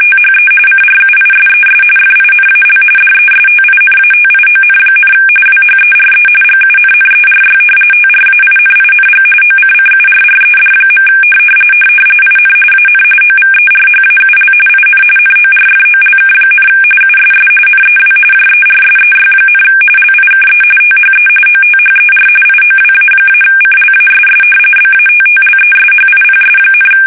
RTTY, about 45.45 baud. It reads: THE CRUISES FINAL DESTINATION, WHERE GUNNERY TRAINING OCCURRED. IN AN OVERHAUL IN NORFOLK, USS NEW JERSEY, LIKE THE OTHER IOWA-CLASS BATTLESHIPS,
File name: fsk850 / battleshipnj